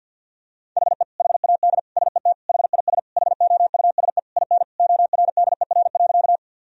Morse code, heard as a keyed tone, about 50 wpm, 700 Hz. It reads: HE HAD SET HIS HOUSE IN ORDER.
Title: Sentences from Top 1000 Words - Review Part 2 - 50wpm